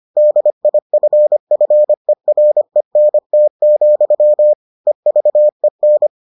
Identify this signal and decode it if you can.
Morse code, heard as a keyed tone, about 25 wpm, 600 Hz. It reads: DIFFERENT, EVEN